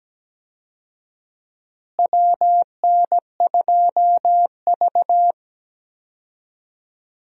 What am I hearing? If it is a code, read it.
Morse code, heard as a keyed tone, about 17 wpm, 700 Hz. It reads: WN2V
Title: USA Call Signs - 4 Characters - Repeated 3x - 17wpm